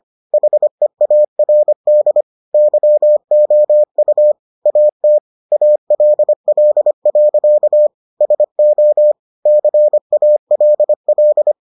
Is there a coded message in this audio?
Yes: Morse code from 0.3 to 11.5 s, heard as a keyed tone, about 25 wpm, 600 Hz. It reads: HEARD YOU AT ALL. SO CALL